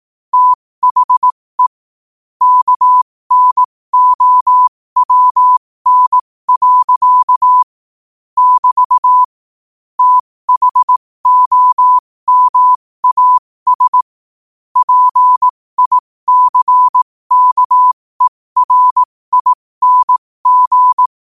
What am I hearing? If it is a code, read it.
Morse code, heard as a keyed tone, about 18 wpm, spaced as for 15 wpm, 1000 Hz. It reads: THE KNOWN. <BT> THOMAS PICKERING
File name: qotdfiles / 15wpm